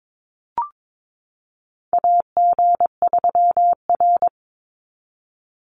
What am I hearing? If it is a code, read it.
Morse code, heard as a keyed tone, about 22 wpm, 700 Hz. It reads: AG3R
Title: Call Signs Repeated 3x - 22wpm